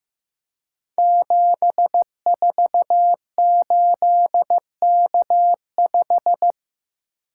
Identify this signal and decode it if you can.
Morse code, heard as a keyed tone, about 15 wpm, 700 Hz. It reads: 748K5
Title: Code Groups - 5 Letter:Numbers - 15wpm